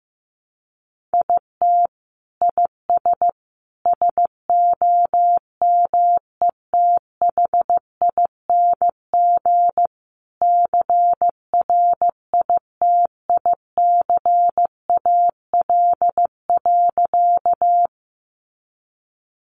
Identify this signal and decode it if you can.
Morse code, heard as a keyed tone, about 15 wpm, 700 Hz. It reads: IT IS SOMETHING CRITICAL.
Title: Sentences from Top 1100 Words 15wpm